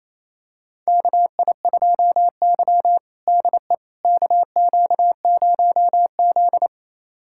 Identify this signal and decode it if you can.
Morse code, heard as a keyed tone, about 28 wpm, 700 Hz. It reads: KI2Y DE KQ0Z
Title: USA Callsign DE Callsign - 4 Characters - 28wpm